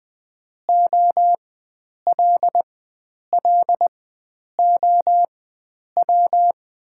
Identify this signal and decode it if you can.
Morse code, heard as a keyed tone, about 20 wpm, spaced as for 9 wpm, 690 Hz. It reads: OLLOW
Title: 4x Character Spacing - Top 300 Words - 20wpm